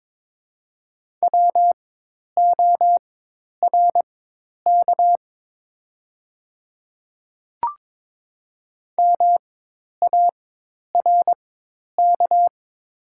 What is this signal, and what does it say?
Morse code, heard as a keyed tone, about 22 wpm, spaced as for 10 wpm, 700 Hz. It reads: WORK MARK
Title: 4x Character Spacing - Top 300 Words - 22wpm